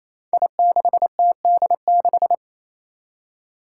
Morse code, heard as a keyed tone, about 27 wpm, 700 Hz. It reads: I6TD6
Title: Five Challenging Letter-Number Combinations 28wpm